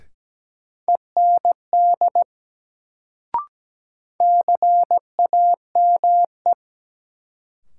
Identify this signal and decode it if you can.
Morse code, heard as a keyed tone, about 17 wpm, 700 Hz. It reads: END CAME
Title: Sentences from Top 200 Words - New Word Drill - 17wpm